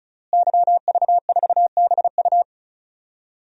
Morse code, heard as a keyed tone, about 36 wpm, 700 Hz. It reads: YV4BU
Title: DXCC-100 Call Signs - 5 Characters - Repeated 3x - 35wpm